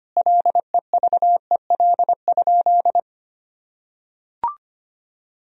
Morse code, heard as a keyed tone, about 25 wpm, 700 Hz. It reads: LEVEL?
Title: Sentences from Top 1000 Words - Review Part 4 - 25wpm